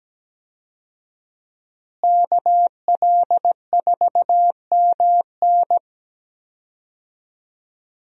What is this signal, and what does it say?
Morse code, heard as a keyed tone, about 17 wpm, 700 Hz. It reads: KL4MN
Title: USA Call Signs - 5 Characters - Repeated 2x - 17wpm